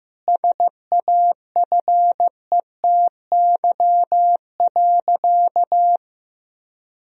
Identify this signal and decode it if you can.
Morse code, heard as a keyed tone, about 15 wpm, 700 Hz. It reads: SAFETY.
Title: Sentences from Top 1000 Words - Review Part 4 - 15wpm